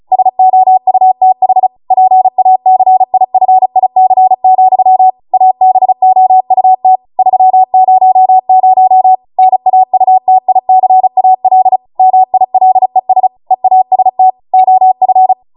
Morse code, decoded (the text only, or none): SOUTH PACIFIC, ABOUT 300 NAUTICAL MILES EAST OF